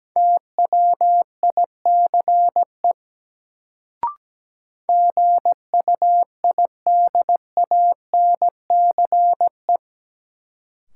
Morse code, heard as a keyed tone, about 17 wpm, 700 Hz. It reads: TWICE GUIDANCE
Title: Sentences from Top 1600 Words - New Word Drill Alt - 17wpm